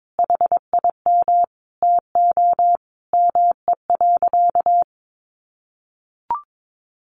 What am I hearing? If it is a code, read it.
Morse code, heard as a keyed tone, about 22 wpm, 700 Hz. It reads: HIM TO ME.
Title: Sentences from Top 1500 Words 22wpm